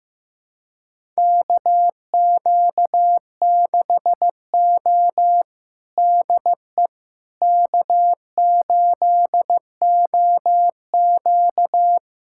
Morse code, heard as a keyed tone, about 15 wpm, 700 Hz. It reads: KQ6O DE K8OQ
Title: USA Callsign DE Callsign - 4 Characters - 15wpm